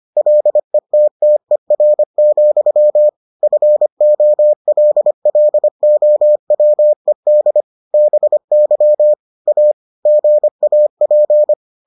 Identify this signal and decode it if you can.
Morse code, heard as a keyed tone, about 25 wpm, 600 Hz. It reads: LETTER, FOLLOWED BY A GAP